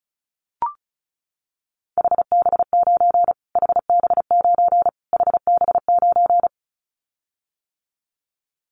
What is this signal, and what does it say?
Morse code, heard as a keyed tone, about 34 wpm, 700 Hz. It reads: HB9 HB9 HB9
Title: Callsign Prefixes for Top 25 DXCC - Repeated 3x - 35wpm